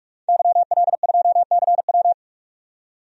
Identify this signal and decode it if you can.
Morse code, heard as a keyed tone, about 46 wpm, 700 Hz. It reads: YL2CW
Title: DXCC-100 Call Signs - 5 Characters - 45wpm